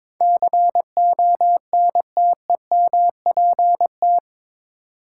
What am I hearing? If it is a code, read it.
Morse code, heard as a keyed tone, about 22 wpm, 700 Hz. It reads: CONTEMPT
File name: practice / 22wpm / Eight Letter Words - 22wpm